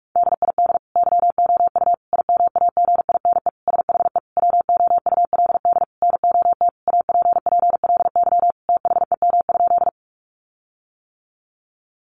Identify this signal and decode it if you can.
Morse code, heard as a keyed tone, about 44 wpm, 700 Hz. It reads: DID YOU IMAGINE SHE WOULD NOT APPLY THEM?
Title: Sentences from Top 1400 Words 45wpm